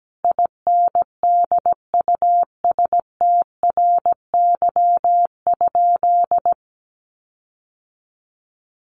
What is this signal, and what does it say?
Morse code, heard as a keyed tone, about 17 wpm, 700 Hz. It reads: INDUSTRY?
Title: Sentences from Top 1300 Words 17wpm